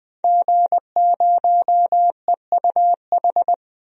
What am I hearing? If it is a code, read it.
Morse code, heard as a keyed tone, about 20 wpm, 700 Hz. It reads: G0EUH
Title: DXCC-100 Call Signs - 5 Characters - 20wpm